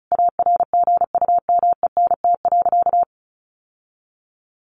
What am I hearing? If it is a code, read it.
Morse code, heard as a keyed tone, about 34 wpm, 700 Hz. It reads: ARGUMENT.
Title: Sentences from Top 1400 Words 35wpm